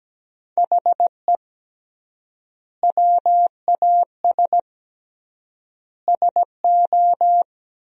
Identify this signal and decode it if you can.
Morse code, heard as a keyed tone, about 17 wpm, 700 Hz. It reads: HE WAS SO
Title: 3x Word-Spacing - Sentences from Top 200 Words - 17wpm